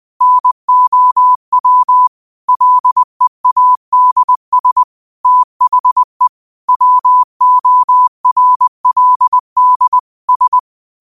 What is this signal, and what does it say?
Morse code, heard as a keyed tone, about 20 wpm, 1000 Hz. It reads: NOW LEADS THE WORLD S